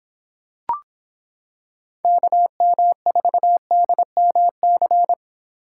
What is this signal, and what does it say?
Morse code, heard as a keyed tone, about 26 wpm, 700 Hz. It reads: KM4DMC